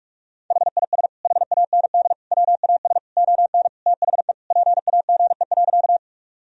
Morse code, heard as a keyed tone, about 46 wpm, 700 Hz. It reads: HIS HAND WAS ON THE PAGE.
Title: Sentences from Top 1000 Words - Review Part 1 - 45wpm